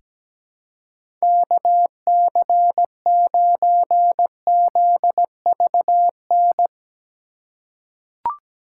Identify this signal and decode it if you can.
Morse code, heard as a keyed tone, about 17 wpm, 700 Hz. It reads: KC9ZVN
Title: USA Call Signs - 6 Characters - 4x Character Spacing - 17wpm